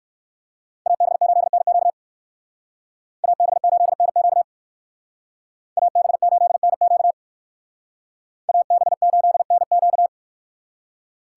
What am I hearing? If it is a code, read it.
Morse code, heard as a keyed tone, about 57 wpm, 700 Hz. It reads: AB8NQ AB8NQ AB8NQ AB8NQ